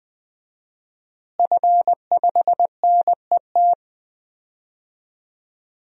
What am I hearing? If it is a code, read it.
Morse code, heard as a keyed tone, about 20 wpm, 700 Hz. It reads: F5NET
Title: DXCC-100 Call Signs - 5 Characters - Repeated 3x - 20wpm